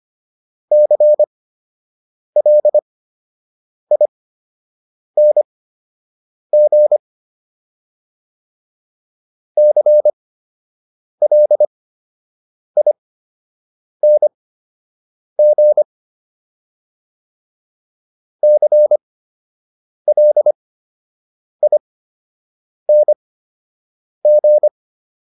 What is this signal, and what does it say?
Morse code, heard as a keyed tone, about 25 wpm, spaced as for 7 wpm, 600 Hz. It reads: CLING CLING CLING